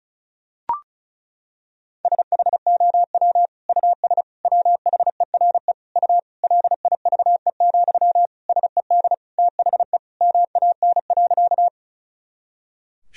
Morse code, heard as a keyed tone, about 34 wpm, 690 Hz. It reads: SHOW US WHERE U LIVE, SED THE MAN.
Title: Sentences from Top 200 Words and Abbreviations 35wpm